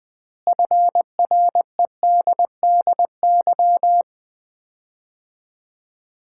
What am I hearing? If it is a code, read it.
Morse code, heard as a keyed tone, about 20 wpm, 700 Hz. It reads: FREDDY
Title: Common QSO Phrases - 4x Word-Spacing - 20wpm